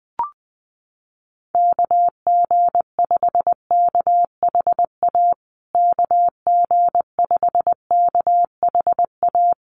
Morse code, heard as a keyed tone, about 19 wpm, 700 Hz. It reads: KG5KHA KG5KHA
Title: USA Call Signs Repeated 3x 20wpm